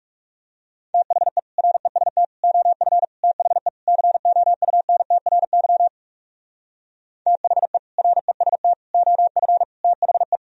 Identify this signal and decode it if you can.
Morse code, heard as a keyed tone, about 40 wpm, 700 Hz. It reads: THE REST OF THE COUNTRY THE REST OF THE